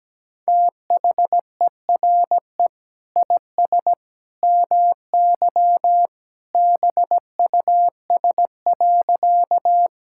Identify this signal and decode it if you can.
Morse code, heard as a keyed tone, about 17 wpm, 700 Hz. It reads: THERE IS MY BUS.
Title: Sentences from Top 1000 Words - Review Part 6 - 17wpm